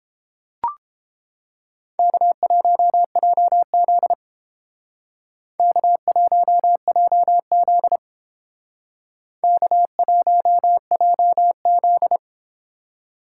Morse code, heard as a keyed tone, about 30 wpm, 700 Hz. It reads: K1JZ K1JZ K1JZ